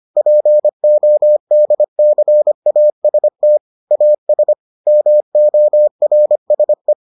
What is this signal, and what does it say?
Morse code, heard as a keyed tone, about 25 wpm, 600 Hz. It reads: PODCAST AS MORSE